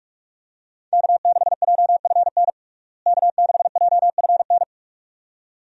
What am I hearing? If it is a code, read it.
Morse code, heard as a keyed tone, about 46 wpm, 700 Hz. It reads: K6JFN K6JFN
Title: USA Call Signs - 5 Characters - Repeated 2x - 45wpm